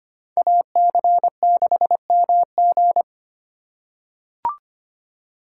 Morse code, heard as a keyed tone, about 25 wpm, 700 Hz. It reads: AC6MG
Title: USA Call Signs - 5 Characters - 4x Character Spacing - 25wpm